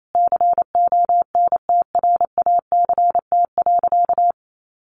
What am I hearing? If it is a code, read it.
Morse code, heard as a keyed tone, about 28 wpm, 700 Hz. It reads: CONTRACT.
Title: Sentences from Top 1400 Words 28wpm